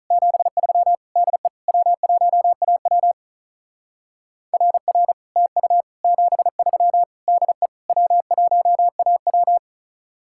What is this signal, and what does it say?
Morse code, heard as a keyed tone, about 44 wpm, 700 Hz. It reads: 73 DE W1AW RR TU 73 DE W1AW